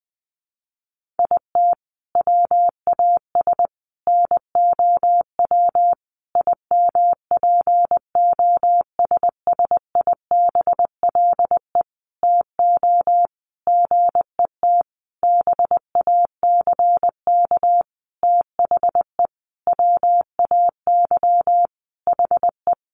Morse code, heard as a keyed tone, about 20 wpm, 700 Hz. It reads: IT WAS NOW IMPOSSIBLE TO GET BACK THE WAY HE
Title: Sentences from Top 1100 Words 20wpm